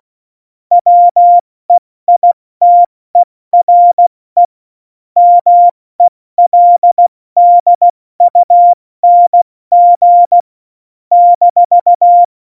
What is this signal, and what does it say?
Morse code, heard as a keyed tone, about 16 wpm, 700 Hz. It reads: WEITERE MELDUNG -
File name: Morsekurs / Witze / Witz 22. Januar 2019, 70 F80